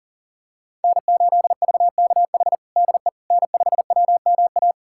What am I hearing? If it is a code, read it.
Morse code, heard as a keyed tone, about 40 wpm, 700 Hz. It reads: N8VKH DE N5WMA
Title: USA Callsign DE Callsign - 5 Characters - 40wpm